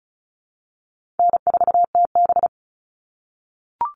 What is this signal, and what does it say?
Morse code, heard as a keyed tone, about 34 wpm, 700 Hz. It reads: N4TB
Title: Call Signs - 35wpm